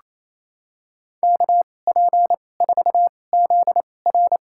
Morse code, heard as a keyed tone, about 28 wpm, spaced as for 20 wpm, 700 Hz. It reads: KP4ZR
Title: USA Call Signs - 5 Characters - 2x Character Spacing - 28wpm